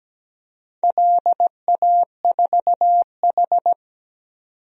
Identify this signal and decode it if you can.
Morse code, heard as a keyed tone, about 17 wpm, 700 Hz. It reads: LA4H
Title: DXCC-100 Call Signs - 4 Characters - 17wpm